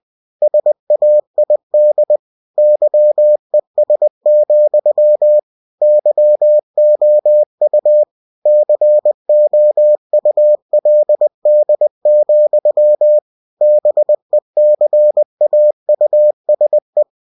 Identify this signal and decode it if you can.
Morse code, heard as a keyed tone, about 20 wpm, 600 Hz. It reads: SAID YES, YOU COULD, BECAUSE
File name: cw-books / winnie-the-pooh / winnie-20wpm-page-0020